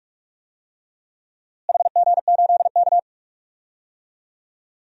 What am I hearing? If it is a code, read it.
Morse code, heard as a keyed tone, about 47 wpm, 700 Hz. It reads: HG8K
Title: DXCC-100 Call Signs - 4 Characters - Repeated 3x - 45wpm